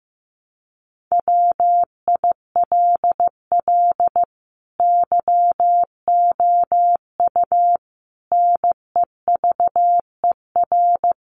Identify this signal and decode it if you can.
Morse code, heard as a keyed tone, about 15 wpm, 700 Hz. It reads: WILL YOU NEVER
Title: Sentences from Top 300 Words 15wpm